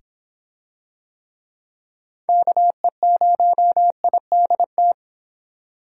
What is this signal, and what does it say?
Morse code, heard as a keyed tone, about 23 wpm, 700 Hz. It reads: KE0IDT